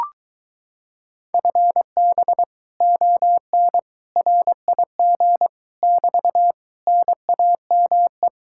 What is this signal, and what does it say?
Morse code, heard as a keyed tone, about 23 wpm, 700 Hz. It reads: FB ON RIG <BT> NAME